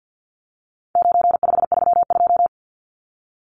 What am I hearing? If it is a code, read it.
Morse code, heard as a keyed tone, about 48 wpm, 700 Hz. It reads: 9532
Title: Numbers - 4-Digit 50wpm